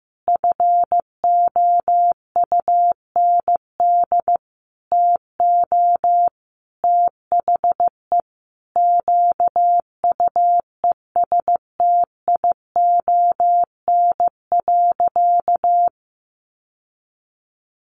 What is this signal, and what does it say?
Morse code, heard as a keyed tone, about 15 wpm, 700 Hz. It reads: FOUND TO THE QUESTION.
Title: Sentences from Top 1400 Words 15wpm